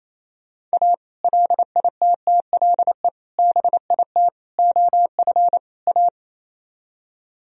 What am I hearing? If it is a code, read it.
Morse code, heard as a keyed tone, about 35 wpm, 700 Hz. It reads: A LITTLE BIT OF A
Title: Speed-Racing - Sets of 5 Words 28wpm